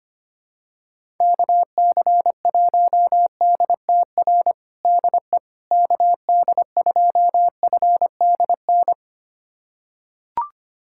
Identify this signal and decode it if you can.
Morse code, heard as a keyed tone, about 25 wpm, 700 Hz. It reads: KC1DTR DE KD2FDN
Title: USA Callsign DE Callsign 25wpm